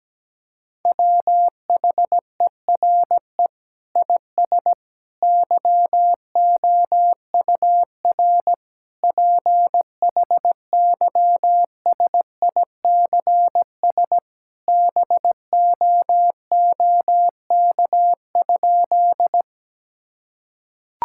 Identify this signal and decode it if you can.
Morse code, heard as a keyed tone, about 17 wpm, 700 Hz. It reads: WHERE IS YOUR PHYSICS BOOK?
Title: Sentences from Top 1000 Words - Review Part 5 - 17wpm